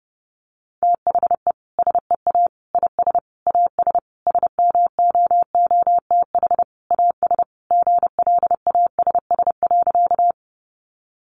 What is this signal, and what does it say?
Morse code, heard as a keyed tone, about 30 wpm, 700 Hz. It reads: THE SEA IS AS SMOOTH AS GLASS.
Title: Sentences from Top 1400 Words 30wpm